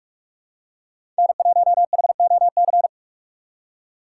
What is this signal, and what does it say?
Morse code, heard as a keyed tone, about 44 wpm, 700 Hz. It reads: N1HOC
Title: USA Call Signs - 5 Characters - 6x Character Spacing - 45wpm